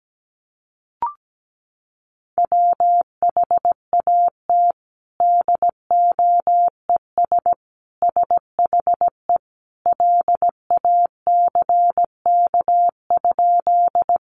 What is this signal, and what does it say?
Morse code, heard as a keyed tone, about 17 wpm, 700 Hz. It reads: WHAT DOES SHE LACK?
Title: Sentences from Top 1100 Words 17wpm